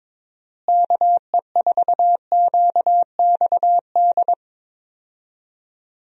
Morse code, heard as a keyed tone, about 22 wpm, 700 Hz. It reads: KE4QXD
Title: USA Call Signs - 6 Characters - Repeated 3x - 22wpm